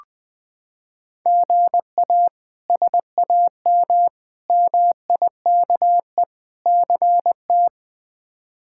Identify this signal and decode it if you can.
Morse code, heard as a keyed tone, about 20 wpm, 700 Hz. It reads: GA SAM MIKE CT
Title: K1USN SST QSO Practice - 20wpm